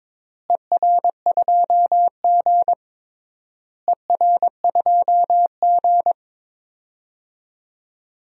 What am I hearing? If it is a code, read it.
Morse code, heard as a keyed tone, about 22 wpm, 700 Hz. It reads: ER2G ER2G
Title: DXCC-100 Call Signs - 4 Characters - Repeated 3x - 22wpm